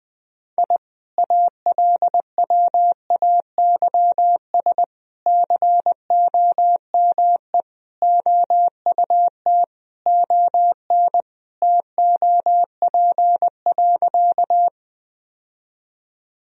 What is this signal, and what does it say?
Morse code, heard as a keyed tone, about 19 wpm, 700 Hz. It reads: I ALWAYS COME OUT ON TOP.
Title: Sentences from Top 1000 Words - Review Part 2 - 20wpm